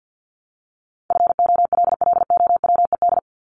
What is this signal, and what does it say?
Morse code, heard as a keyed tone, about 54 wpm, 700 Hz. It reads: FOLLOWED
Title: ED Suffix - Top 100 Words with Base Words 50wpm